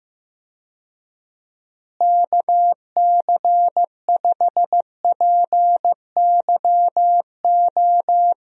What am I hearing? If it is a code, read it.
Morse code, heard as a keyed tone, about 15 wpm, 700 Hz. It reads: KC5PYO
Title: USA Call Signs - 6 Characters - Repeated 3x - 15wpm